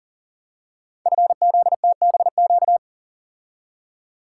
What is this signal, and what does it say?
Morse code, heard as a keyed tone, about 42 wpm, 690 Hz. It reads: FZTBQ
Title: Code Groups - 5 Letters - 40wpm